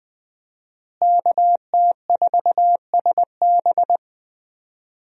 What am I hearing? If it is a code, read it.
Morse code, heard as a keyed tone, about 20 wpm, 700 Hz. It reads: KT4SB
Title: USA Call Signs - 5 Characters - Repeated 3x - 20wpm